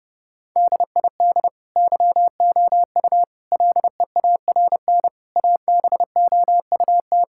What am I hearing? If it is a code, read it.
Morse code, heard as a keyed tone, about 30 wpm, 700 Hz. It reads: DID YOU LEARN ABOUT
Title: Sentences from Top 1000 Words - Review Part 5 - 30wpm